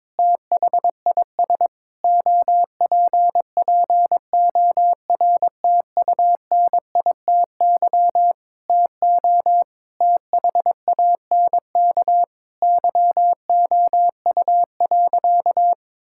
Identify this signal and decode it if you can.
Morse code, heard as a keyed tone, about 22 wpm, 700 Hz. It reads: THIS OPPORTUNITY TO THANK YOU.
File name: Sentences with 7-Word Phrases - 22wpm